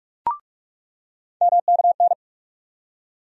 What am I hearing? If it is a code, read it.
Morse code, heard as a keyed tone, about 46 wpm, 700 Hz. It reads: MKN
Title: USA Call Sign Postfixes - 3 Letters 45wpm